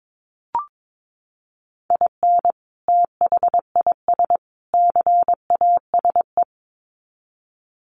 Morse code, heard as a keyed tone, about 22 wpm, 700 Hz. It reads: IN THIS CASE
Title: Sets of 3 Words 22wpm